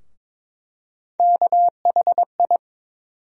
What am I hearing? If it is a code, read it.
Morse code, heard as a keyed tone, about 22 wpm, 700 Hz. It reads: KHI
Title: USA Call Sign Postfixes - 3 Letters 22wpm